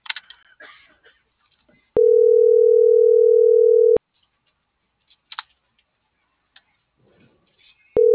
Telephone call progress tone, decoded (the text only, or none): ringback tone